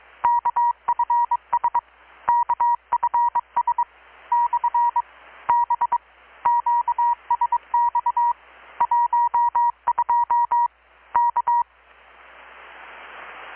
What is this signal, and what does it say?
Morse code, heard as a keyed tone, about 22 wpm, 950 Hz. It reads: KFS KFS / B QSX 12 K
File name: blog / swl / 6-abril-2019-12695-USB-KPS